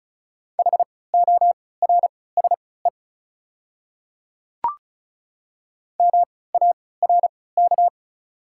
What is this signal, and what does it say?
Morse code, heard as a keyed tone, about 36 wpm, spaced as for 20 wpm, 700 Hz. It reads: HORSE MARK
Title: 3x Character Spacing - Top 300 Words - 35wpm